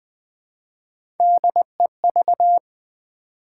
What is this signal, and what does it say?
Morse code, heard as a keyed tone, about 20 wpm, 700 Hz. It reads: DEV
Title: Build A Word - 1 Letter At A Time - 3 to 5-Letter Words - 20wpm